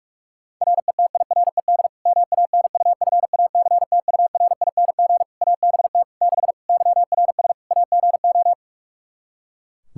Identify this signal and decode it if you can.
Morse code, heard as a keyed tone, about 41 wpm, 700 Hz. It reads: RETIRED MANUFACTURING ABT 6 YRS AGO
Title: Practice QSO - Exchange 3 - More Info - 45wpm